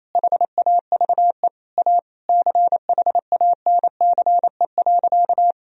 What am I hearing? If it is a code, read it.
Morse code, heard as a keyed tone, about 28 wpm, 700 Hz. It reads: HAVE A CHANCE.
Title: Sentences from Top 1000 Words - Review Part 2 - 28wpm